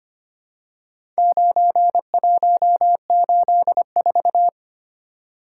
Morse code, heard as a keyed tone, about 25 wpm, 700 Hz. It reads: 9184
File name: practice / 25wpm / Four Letter-Number Combinations - 25wpm